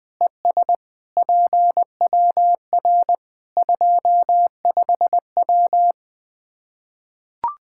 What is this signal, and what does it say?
Morse code, heard as a keyed tone, about 20 wpm, 700 Hz. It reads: ES PWR 25W
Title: Practice QSO - Exchange 2 - Info - 20wpm